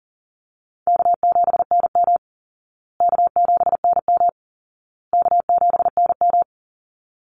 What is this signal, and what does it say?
Morse code, heard as a keyed tone, about 40 wpm, 700 Hz. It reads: K7NM K7NM K7NM
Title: Easy Call Signs Repeated 3x - 40wpm